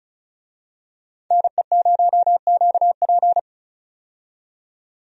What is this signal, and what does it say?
Morse code, heard as a keyed tone, about 36 wpm, 700 Hz. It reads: NE0QP
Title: USA Call Signs - 5 Characters - 35wpm